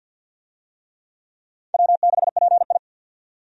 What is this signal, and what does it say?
Morse code, heard as a keyed tone, about 51 wpm, 700 Hz. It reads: W6PI
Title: USA Call Signs - 4 Characters - 50wpm